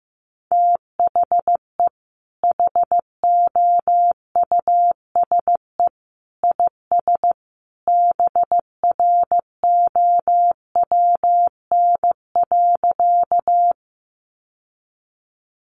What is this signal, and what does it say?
Morse code, heard as a keyed tone, about 15 wpm, 700 Hz. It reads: THE HOUSE IS BROWN.
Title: Sentences from Top 1200 Words 15wpm